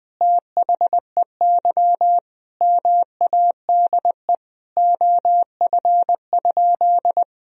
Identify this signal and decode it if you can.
Morse code, heard as a keyed tone, about 21 wpm, 700 Hz. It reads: THEY MADE OF?
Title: Sentences from Top 200 Words and Abbreviations 20wpm